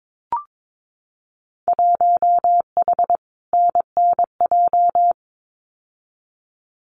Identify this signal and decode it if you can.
Morse code, heard as a keyed tone, about 22 wpm, 700 Hz. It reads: 1H NNJ
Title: Winter Field Day Exchanges 22wpm